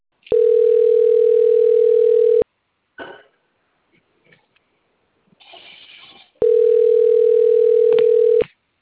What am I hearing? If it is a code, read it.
Ringback tone (North America), 440 + 480 Hz.